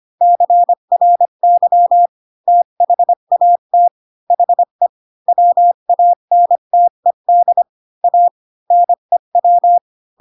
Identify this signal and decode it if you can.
Morse code, heard as a keyed tone, about 25 wpm, 700 Hz. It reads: CRY THAT HE WANTED A NEW
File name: Pinocchio / Pinocchio ch8 20 25 ik1hge